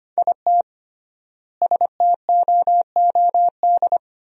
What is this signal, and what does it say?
Morse code, heard as a keyed tone, about 25 wpm, 700 Hz. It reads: IT STOOD
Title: 3x Word-Spacing - Sentences from Top 500 Words - 25wpm